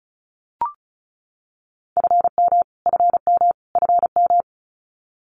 Morse code, heard as a keyed tone, about 34 wpm, 700 Hz. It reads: FM FM FM
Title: Abbreviations Repeated 3x - 35wpm